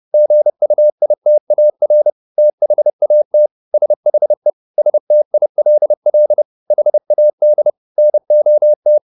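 Morse code, heard as a keyed tone, about 30 wpm, 600 Hz. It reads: GUITAR THAT SHE STILL HAD NOT
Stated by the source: Morse code at 30 wpm, the novel Never by Ken Follett in English, from CW med Gnister.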